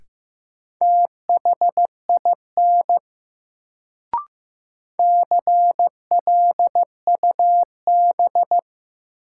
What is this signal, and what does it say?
Morse code, heard as a keyed tone, about 15 wpm, 700 Hz. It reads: THIN CLUB
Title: Sentences from Top 1200 Words - New Word Drill - 15wpm